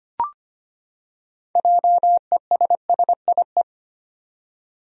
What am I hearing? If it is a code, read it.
Morse code, heard as a keyed tone, about 25 wpm, 700 Hz. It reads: JESSIE
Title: USA Common Names - 25wpm